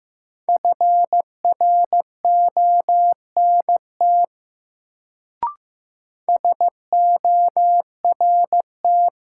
Morse code, heard as a keyed tone, about 15 wpm, 700 Hz. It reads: FRONT SORT
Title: Sets of 2 Words - New Word Drill - 15wpm